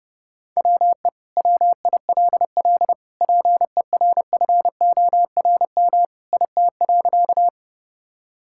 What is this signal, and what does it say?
Morse code, heard as a keyed tone, about 30 wpm, 700 Hz. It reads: WE WILL PERFORM IT.
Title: Sentences from Top 1000 Words - Review Part 5 - 30wpm